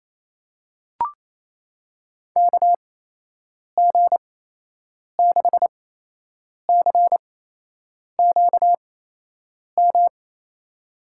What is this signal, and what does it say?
Morse code, heard as a keyed tone, about 28 wpm, spaced as for 8 wpm, 700 Hz. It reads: KG6CQM
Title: USA Call Signs - 6 Characters - 8x Character Spacing - 28wpm